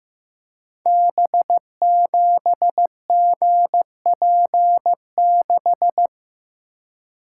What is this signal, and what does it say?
Morse code, heard as a keyed tone, about 15 wpm, 700 Hz. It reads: B7GP6
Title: Code Groups - 5 Letter:Numbers - 15wpm